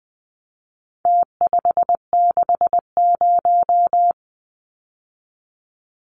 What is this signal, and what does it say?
Morse code, heard as a keyed tone, about 20 wpm, 700 Hz. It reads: T560
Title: Numbers - Contest Style - 4-Digit 20wpm